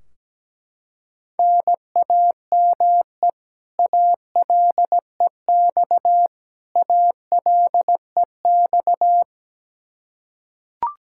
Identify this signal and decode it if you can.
Morse code, heard as a keyed tone, about 17 wpm, 700 Hz. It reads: NAME ALEX ALEX
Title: Practice QSO - Exchange 1 - Beginning - 17wpm